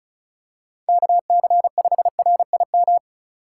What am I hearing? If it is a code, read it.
Morse code, heard as a keyed tone, about 36 wpm, 700 Hz. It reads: KC5RIM
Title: USA Call Signs - 6 Characters - 8x Character Spacing - 35wpm